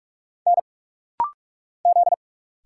Morse code, heard as a keyed tone, about 46 wpm, 700 Hz. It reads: N Z
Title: Single Letters - Rapid-Fire 45wpm